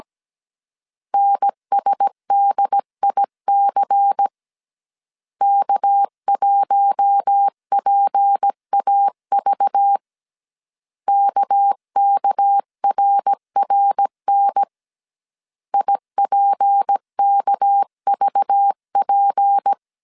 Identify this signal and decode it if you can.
Morse code, heard as a keyed tone, about 17 wpm, 800 Hz. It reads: NSDIC K1PAV KKRRN IPKVP
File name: CW / DET / 1 6 1500C S C 16wpm 2733